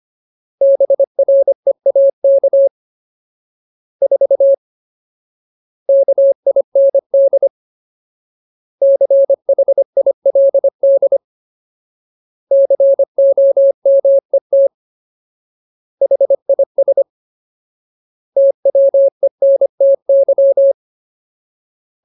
Morse code, heard as a keyed tone, about 25 wpm, 550 Hz. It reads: BREAK 4 KIND CHILD COMET HIS TWENTY
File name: Ultimate Word List / 25WPM Ultimate